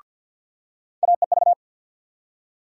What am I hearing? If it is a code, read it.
Morse code, heard as a keyed tone, about 52 wpm, 700 Hz. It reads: AEV
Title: USA Call Sign Postfixes - 3 Letters 50wpm